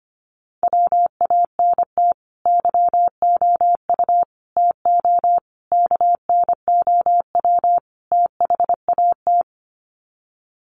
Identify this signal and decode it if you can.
Morse code, heard as a keyed tone, about 25 wpm, 700 Hz. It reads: WANT YOU TO KNOW THAT
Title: Sets of 5 Words Encore 25wpm